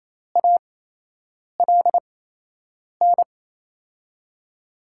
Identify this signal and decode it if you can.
Morse code, heard as a keyed tone, about 28 wpm, spaced as for 8 wpm, 700 Hz. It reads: ALN
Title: USA Call Signs - 8x Character Spacing - 28wpm